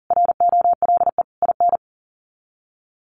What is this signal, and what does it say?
Morse code, heard as a keyed tone, about 40 wpm, 700 Hz. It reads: ROLE IN
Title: Sets of 5 Words Encore2 40wpm